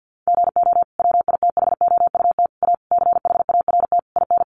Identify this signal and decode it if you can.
Morse code, heard as a keyed tone, about 50 wpm, 700 Hz. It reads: GO WITHOUT A CHART IN
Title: Sentences from Top 1300 Words 50wpm